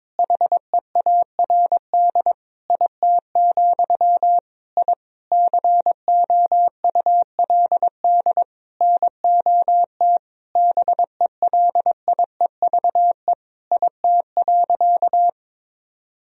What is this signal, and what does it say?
Morse code, heard as a keyed tone, about 22 wpm, 700 Hz. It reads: HEARD IT, I COULD NOT BELIEVE IT.
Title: Sentences from Top 1000 Words - Review Part 3 - 22wpm